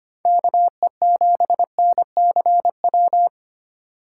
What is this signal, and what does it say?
Morse code, heard as a keyed tone, about 25 wpm, 700 Hz. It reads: KE7NCW
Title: USA Call Signs - 6 Characters - 2x Character Spacing - 25wpm